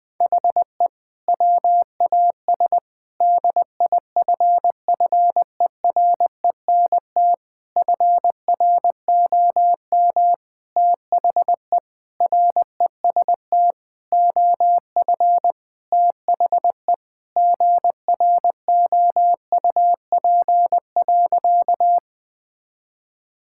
Morse code, heard as a keyed tone, about 21 wpm, 700 Hz. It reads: HE WAS DIFFERENT FROM THE REST OF THE GROUP.
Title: Sentences with 5-Word Phrases - 20wpm